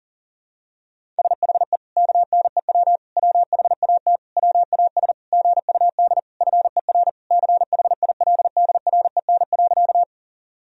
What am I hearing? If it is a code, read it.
Morse code, heard as a keyed tone, about 40 wpm, 700 Hz. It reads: SHE KNEW WHAT WAS GUD FER CHILDREN.
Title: Sentences from Top 400 Words and Abbreviations 40wpm